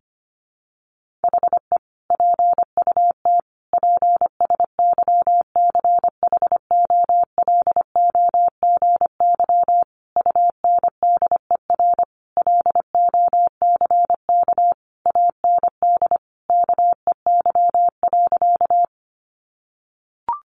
Morse code, heard as a keyed tone, about 25 wpm, 700 Hz. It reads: HE PUT PSYCHOLOGY UNDER LOCK AND KEY.
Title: Sentences from Top 1400 Words 25wpm